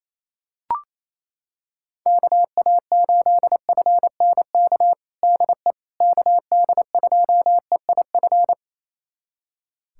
Morse code, heard as a keyed tone, about 28 wpm, 700 Hz. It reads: KA8FNK DE KD2EIF
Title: USA Callsign DE Callsign - 6 Characters - 28wpm